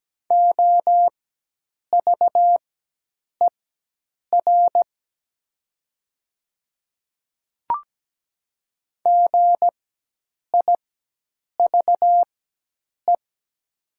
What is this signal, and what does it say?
Morse code, heard as a keyed tone, about 17 wpm, spaced as for 8 wpm, 700 Hz. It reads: OVER GIVE
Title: 4x Character Spacing - Top 200 Words - 17wpm